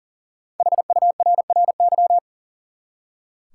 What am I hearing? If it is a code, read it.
Morse code, heard as a keyed tone, about 40 wpm, 700 Hz. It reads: HURRY
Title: Five Letter Words - 40wpm